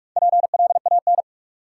Morse code, heard as a keyed tone, about 44 wpm, 700 Hz. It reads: PLAN
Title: Sentences from Top 500 Words - New Word Drill - 45wpm